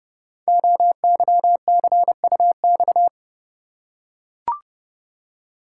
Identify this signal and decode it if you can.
Morse code, heard as a keyed tone, about 30 wpm, 700 Hz. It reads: OYCUX
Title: Code Groups - 5 Letter:Numbers - 30wpm